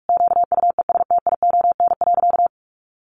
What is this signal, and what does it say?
Morse code, heard as a keyed tone, about 44 wpm, 700 Hz. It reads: QUESTION.
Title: Sentences from Top 600 Words 45wpm